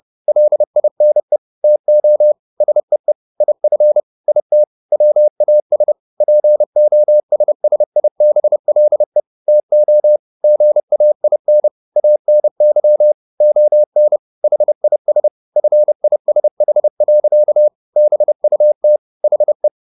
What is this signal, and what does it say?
Morse code, heard as a keyed tone, about 30 wpm, 600 Hz. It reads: LINE TO SEE IF IT WAS POSSIBLE TO GAIN ANY ON HIS FISH. BUT HE
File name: the old man and the sea 2 30wpm